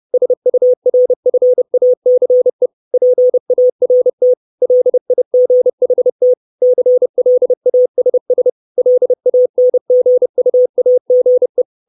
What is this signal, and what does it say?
Morse code, heard as a keyed tone, about 30 wpm, 500 Hz. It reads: SURFACE PART LIGHT CLASS LANGUAGE
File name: 632 Words 30WPM